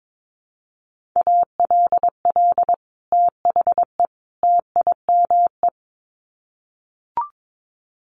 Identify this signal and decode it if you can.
Morse code, heard as a keyed tone, about 22 wpm, 700 Hz. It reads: ALL THE TIME
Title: Sets of 3 Words 22wpm